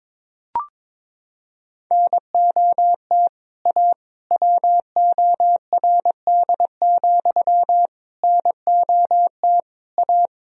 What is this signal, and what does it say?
Morse code, heard as a keyed tone, about 22 wpm, 700 Hz. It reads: NOT A WORD, NOT A
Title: Sentences from Top 1000 Words - Review Part 1 - 22wpm